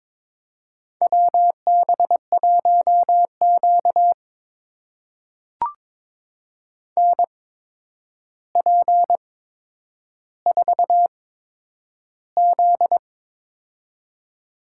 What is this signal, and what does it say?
Morse code, heard as a keyed tone, about 23 wpm, spaced as for 10 wpm, 700 Hz. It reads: WB1Q NP4Z
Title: USA Call Signs - 4 Characters - 8x Character Spacing - 22wpm